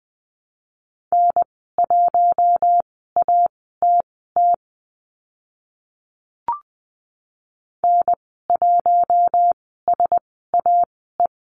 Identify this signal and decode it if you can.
Morse code, heard as a keyed tone, about 20 wpm, spaced as for 14 wpm, 700 Hz. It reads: N1ATT N1SAE
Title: 042 Lesson 8g - Number 1 - Call Signs 20wpm